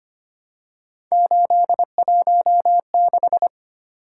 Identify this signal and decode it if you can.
Morse code, heard as a keyed tone, about 25 wpm, 700 Hz. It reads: 816
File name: Three Letter-Number Combinations - 25wpm